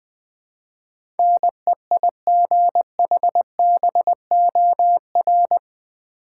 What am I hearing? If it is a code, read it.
Morse code, heard as a keyed tone, about 20 wpm, 700 Hz. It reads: NEIGHBOR
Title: Build A Word - 4 Letters At A Time - 11-Letter Words - 20wpm